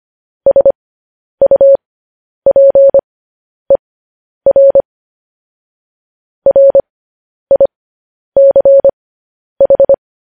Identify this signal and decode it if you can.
Morse code, heard as a keyed tone, about 25 wpm, spaced as for 10 wpm, 550 Hz. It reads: SUPER RICH